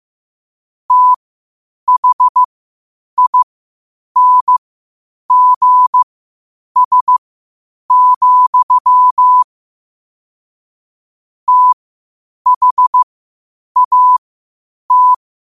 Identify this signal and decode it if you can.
Morse code, heard as a keyed tone, about 15 wpm, spaced as for 8 wpm, 1000 Hz. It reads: THINGS, THAT
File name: qotdfiles / 8wpm